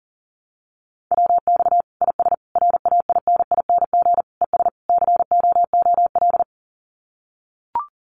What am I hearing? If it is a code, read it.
Morse code, heard as a keyed tone, about 40 wpm, 700 Hz. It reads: WX IS RAINING ES COOL
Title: Common QSO Phrases 40wpm